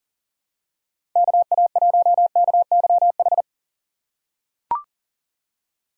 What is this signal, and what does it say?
Morse code, heard as a keyed tone, about 40 wpm, 700 Hz. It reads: KA1KYH
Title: USA Call Signs - 6 Characters - 2x Character Spacing - 40wpm